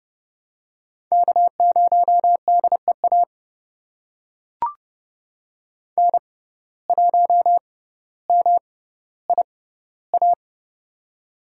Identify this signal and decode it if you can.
Morse code, heard as a keyed tone, about 30 wpm, spaced as for 15 wpm, 700 Hz. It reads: K0DEA N1MIA
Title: USA Call Signs - 6x Character Spacing - 30wpm